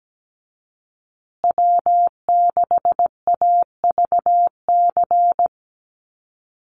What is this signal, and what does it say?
Morse code, heard as a keyed tone, about 17 wpm, 700 Hz. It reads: W6AVC
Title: USA Call Signs Repeated 3x 17wpm